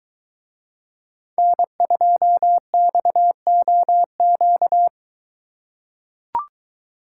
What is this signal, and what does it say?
Morse code, heard as a keyed tone, about 22 wpm, 700 Hz. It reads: N2XOQ